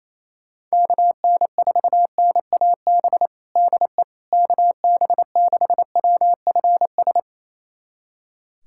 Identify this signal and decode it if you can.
Morse code, heard as a keyed tone, about 28 wpm, 700 Hz. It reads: KN4NAB DE KB6WFS
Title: USA Callsign DE Callsign - 6 Characters - 28wpm